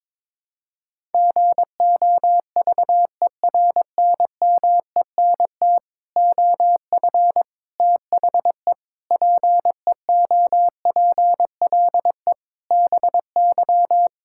Morse code, heard as a keyed tone, about 22 wpm, 700 Hz. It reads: GOVERNMENT OF THE PEOPLE BY
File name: Sets of 9 Words - 22wpm